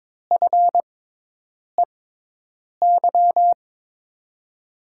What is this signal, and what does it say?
Morse code, heard as a keyed tone, about 22 wpm, spaced as for 8 wpm, 700 Hz. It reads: FEY
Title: USA Call Signs - 6 Characters - 6x Character Spacing - 22wpm